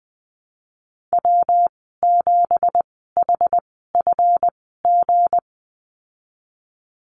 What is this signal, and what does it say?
Morse code, heard as a keyed tone, about 20 wpm, spaced as for 15 wpm, 700 Hz. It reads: W7HFG
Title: 179 Lesson 29g - Slash - Call Signs 20wpm